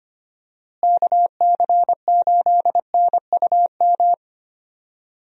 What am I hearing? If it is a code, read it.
Morse code, heard as a keyed tone, about 25 wpm, 700 Hz. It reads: KC8NUM
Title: USA Call Signs - 6 Characters - Repeated 3x - 25wpm